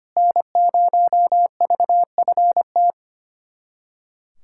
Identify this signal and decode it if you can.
Morse code, heard as a keyed tone, about 25 wpm, 700 Hz. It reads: N0VFT
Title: USA Callsign DE Callsign 25wpm